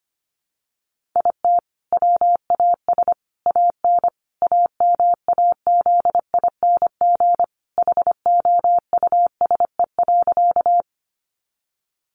Morse code, heard as a keyed tone, about 25 wpm, 700 Hz. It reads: IT WAS AN AMAZING HOUSE.
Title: Sentences from Top 1500 Words 25wpm